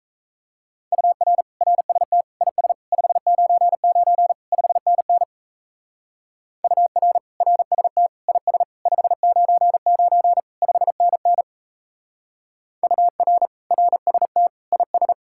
Morse code, heard as a keyed tone, about 34 wpm, 700 Hz. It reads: UR RST IS 599 5NN UR RST IS 599 5NN UR RST IS
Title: Speed-Racing Common QSO Phrases - 28wpm